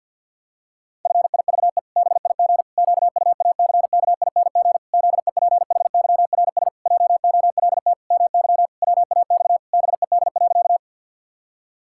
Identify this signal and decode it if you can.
Morse code, heard as a keyed tone, about 50 wpm, 700 Hz. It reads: FIVE BIG QUACKING ZEPHYRS JOLT MY WAX BED.
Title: Pangram Challenge v2 - 50wpm